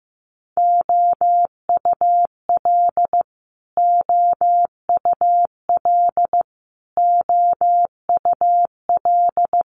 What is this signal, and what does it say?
Morse code, heard as a keyed tone, about 15 wpm, 700 Hz. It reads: OUL OUL OUL
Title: Multi-letter Phonemes Repeated 3x 15wpm